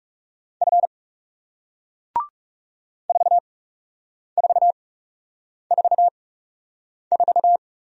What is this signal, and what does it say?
Morse code, heard as a keyed tone, about 40 wpm, spaced as for 8 wpm, 700 Hz. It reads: F 4444